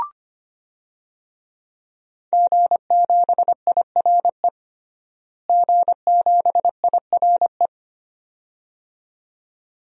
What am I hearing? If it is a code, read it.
Morse code, heard as a keyed tone, about 25 wpm, 700 Hz. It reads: G7IRE G7IRE
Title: DXCC-100 Call Signs - 5 Characters - Repeated 2x - 25wpm